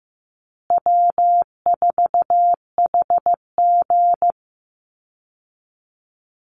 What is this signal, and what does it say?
Morse code, heard as a keyed tone, about 15 wpm, 700 Hz. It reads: W4HG
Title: Easy Call Signs Repeated 2x - 15wpm